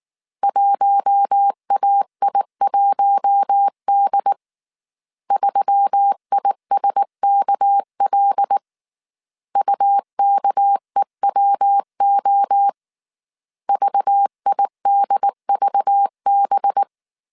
Morse code, heard as a keyed tone, about 19 wpm, 800 Hz. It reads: 1AI1D 3ISKL UKEWO VIDVB